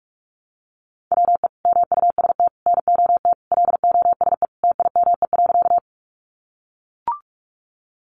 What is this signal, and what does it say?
Morse code, heard as a keyed tone, about 44 wpm, 700 Hz. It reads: WE MUST NOT LOSE TIME.
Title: Sentences from Top 1100 Words 45wpm